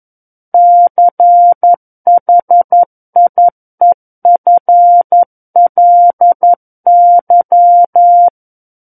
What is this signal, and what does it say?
Morse code, heard as a keyed tone, about 11 wpm, 700 Hz. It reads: CHIEFLY
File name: 02-Chapter 2 - 11 WPM